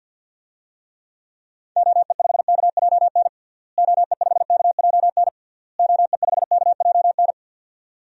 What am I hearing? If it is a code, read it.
Morse code, heard as a keyed tone, about 52 wpm, 700 Hz. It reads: OE5KJN OE5KJN OE5KJN
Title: DXCC-100 Call Signs - 6 Characters - Repeated 3x - 50wpm